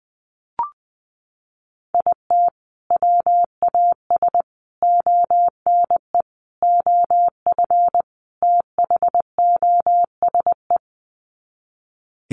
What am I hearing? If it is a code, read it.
Morse code, heard as a keyed tone, about 20 wpm, 700 Hz. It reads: IT WAS ONE OF THOSE
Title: Sets of 5 Words Encore 20wpm